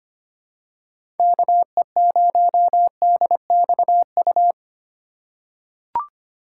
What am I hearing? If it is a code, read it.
Morse code, heard as a keyed tone, about 25 wpm, 700 Hz. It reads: KE0DXU
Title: USA Call Signs - 6 Characters - 3x Character Spacing - 25wpm